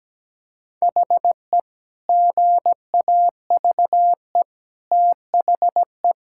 Morse code, heard as a keyed tone, about 17 wpm, 700 Hz. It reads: HE GAVE THE
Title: Sentences from Top 1600 Words 17wpm